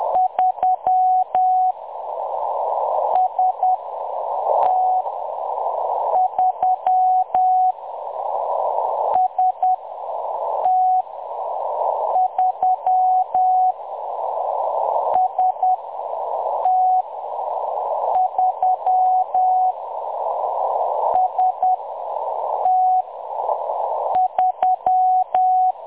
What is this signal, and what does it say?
Morse code, heard as a keyed tone, about 10 wpm, spaced as for 6 wpm, 740 Hz. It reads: 3 ST3 ST3 ST3 ST3
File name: M23-ST3- -0858utc-10184khz